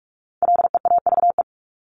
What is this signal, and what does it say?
Morse code, heard as a keyed tone, about 44 wpm, 700 Hz. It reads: LEAVE
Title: Top 500 Words 45wpm